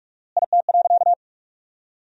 Morse code, heard as a keyed tone, about 46 wpm, 700 Hz. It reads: IT.
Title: 4x Word-Spacing - Sentences from Top 500 Words - 45wpm